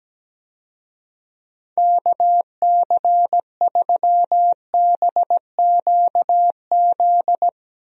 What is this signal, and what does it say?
Morse code, heard as a keyed tone, about 17 wpm, 700 Hz. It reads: KC3BQZ
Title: USA Call Signs - 6 Characters - Repeated 2x - 17wpm